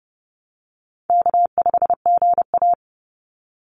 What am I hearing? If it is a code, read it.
Morse code, heard as a keyed tone, about 30 wpm, 700 Hz. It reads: K5GA
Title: Call Signs - 30wpm